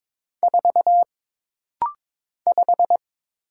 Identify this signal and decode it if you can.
Morse code, heard as a keyed tone, about 22 wpm, 700 Hz. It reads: 4 5
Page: This rapid-fire timing is geared to push your ICR (Instant Character Recognition) skills.